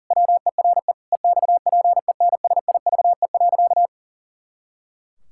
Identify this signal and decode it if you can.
Morse code, heard as a keyed tone, about 37 wpm, 700 Hz. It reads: WERE EXPENSIVE.
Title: Pangram Challenge v2 - 40wpm